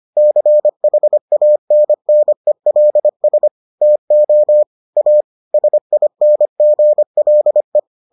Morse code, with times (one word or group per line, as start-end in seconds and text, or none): 0.2-3.5 CHANNELS
3.8-4.6 TO
5.0-5.2 A
5.5-7.8 SINGLE